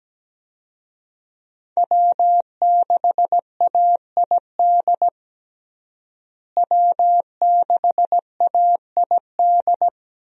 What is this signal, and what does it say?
Morse code, heard as a keyed tone, about 17 wpm, 700 Hz. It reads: W6AID W6AID
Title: USA Call Signs - 5 Characters - Repeated 3x - 17wpm